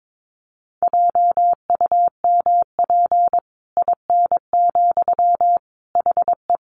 Morse code, heard as a keyed tone, about 22 wpm, 700 Hz. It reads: JUMP IN, HE
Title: Sentences from Top 1300 Words 22wpm